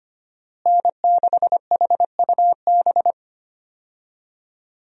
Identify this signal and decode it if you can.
Morse code, heard as a keyed tone, about 25 wpm, 700 Hz. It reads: N6HUB
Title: USA Call Signs - 5 Characters - Repeated 3x - 25wpm